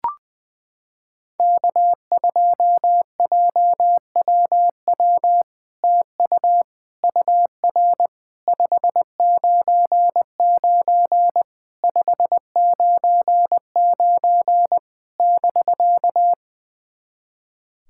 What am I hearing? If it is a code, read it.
Morse code, heard as a keyed tone, about 20 wpm, 700 Hz. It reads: K2JWW TU UR 599 599 <BK>
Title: SOTA QSO Practice 20wpm